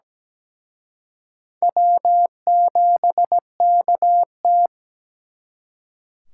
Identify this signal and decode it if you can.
Morse code, heard as a keyed tone, about 20 wpm, 700 Hz. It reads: W7KT
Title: Speed Racing - USA 1x2 Call Signs 17wpm